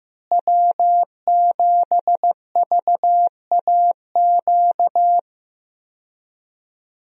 Morse code, heard as a keyed tone, about 15 wpm, 700 Hz. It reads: W7VAQ
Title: USA Call Signs - 5 Characters - Repeated 2x - 15wpm